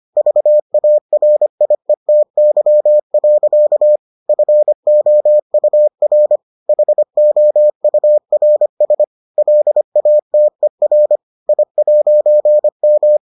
Morse code, heard as a keyed tone, about 25 wpm, 600 Hz. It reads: VARIETY. FOUR HOURS LATER I'M